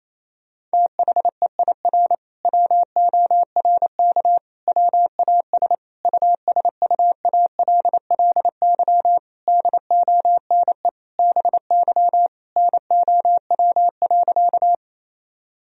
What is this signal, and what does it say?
Morse code, heard as a keyed tone, about 28 wpm, 700 Hz. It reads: THEIR WORK WAS USUALLY DONE BY NOW.
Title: Sentences from Top 1000 Words - Review Part 3 - 28wpm